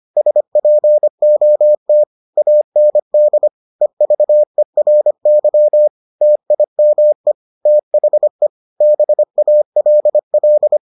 Morse code, heard as a keyed tone, about 25 wpm, 600 Hz. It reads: SPOT AND EVERY TIME THE BALL